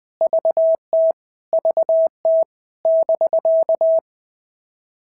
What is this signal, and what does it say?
Morse code, heard as a keyed tone, about 20 wpm, 650 Hz. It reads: VT VT <BK>
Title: POTA QSO Practice - 20wpm